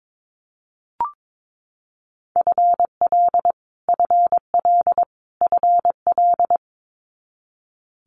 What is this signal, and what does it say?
Morse code, heard as a keyed tone, about 22 wpm, 700 Hz. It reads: FL FL FL
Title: Multi-letter Phonemes Repeated 3x 22wpm